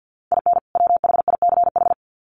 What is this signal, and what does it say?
Morse code, heard as a keyed tone, about 50 wpm, 700 Hz. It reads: IN WHICH
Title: Sets of 2 Words 50wpm